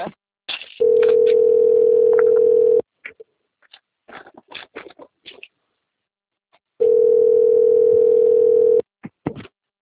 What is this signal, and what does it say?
Ringback tone (North America), 440 + 480 Hz.
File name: out- - - - - .25372